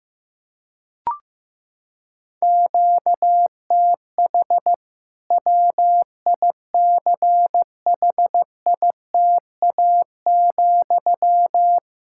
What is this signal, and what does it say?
Morse code, heard as a keyed tone, about 15 wpm, 700 Hz. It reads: QTH WICHITA,
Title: K1USN SST QSO Practice 15wpm